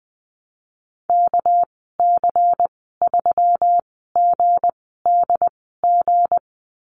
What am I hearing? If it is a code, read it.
Morse code, heard as a keyed tone, about 20 wpm, spaced as for 14 wpm, 700 Hz. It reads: KC3GDG
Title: 213 Lesson 34g - Number 8 - Call Signs 20wpm